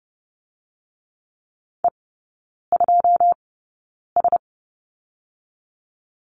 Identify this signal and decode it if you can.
Morse code, heard as a keyed tone, about 30 wpm, spaced as for 9 wpm, 700 Hz. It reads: E2S
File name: Lesson 19e - Letter w - Three Characters 30wpm